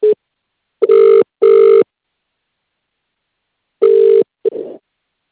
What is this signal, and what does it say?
Ringback tone (UK), 400 + 450 Hz.